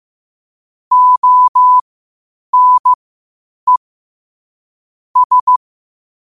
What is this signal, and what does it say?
Morse code, heard as a keyed tone, about 15 wpm, spaced as for 8 wpm, 1000 Hz. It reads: ONE S
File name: qotdfiles / 8wpm